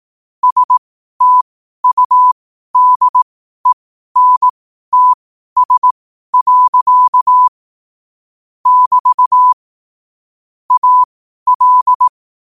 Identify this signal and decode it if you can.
Morse code, heard as a keyed tone, about 18 wpm, spaced as for 12 wpm, 1000 Hz. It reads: STUDENTS. <BT> AL
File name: qotdfiles / 12wpm